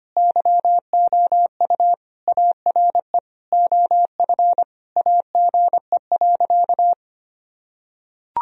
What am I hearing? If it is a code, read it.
Morse code, heard as a keyed tone, about 25 wpm, 700 Hz. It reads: YOU ARE OF AGE.
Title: Sentences from Top 1000 Words - Review Part 3 - 25wpm